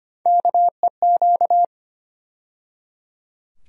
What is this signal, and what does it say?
Morse code, heard as a keyed tone, about 25 wpm, 700 Hz. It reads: KEQ
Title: USA Call Sign Postfixes - 3 Letters 25wpm